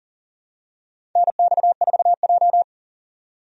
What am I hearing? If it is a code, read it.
Morse code, heard as a keyed tone, about 37 wpm, 700 Hz. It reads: NX4J
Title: USA Call Signs - 4 Characters - 40wpm